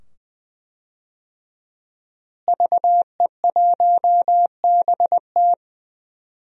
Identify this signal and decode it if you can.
Morse code, heard as a keyed tone, about 20 wpm, 700 Hz. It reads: VE1BT
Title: DXCC-100 Call Signs - 5 Characters - Repeated 2x - 20wpm